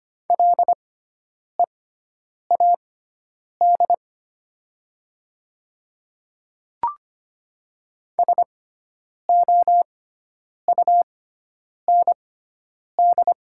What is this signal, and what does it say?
Morse code, heard as a keyed tone, about 25 wpm, spaced as for 9 wpm, 700 Hz. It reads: LEAD SOUND
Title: 6x Character Spacing - Top 500 Words - 25wpm